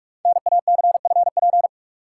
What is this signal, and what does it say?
Morse code, heard as a keyed tone, about 44 wpm, 700 Hz. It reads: NACFP
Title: Code Groups - 5 Letter:Numbers - 45wpm